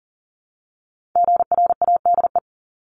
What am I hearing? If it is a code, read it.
Morse code, heard as a keyed tone, about 42 wpm, 700 Hz. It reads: GRADE
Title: Multi-letter Phonemes Repeated 3x 40wpm